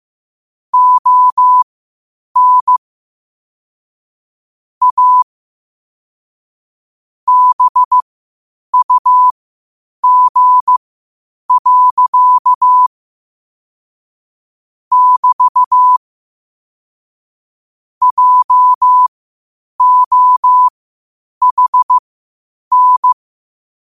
Morse code, heard as a keyed tone, about 15 wpm, spaced as for 8 wpm, 1000 Hz. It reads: ON A BUG. <BT> JOHN
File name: qotdfiles / 8wpm